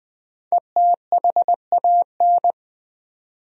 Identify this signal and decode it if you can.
Morse code, heard as a keyed tone, about 20 wpm, 700 Hz. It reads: ETHAN
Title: USA Common Names - Max 5 Letters - 20wpm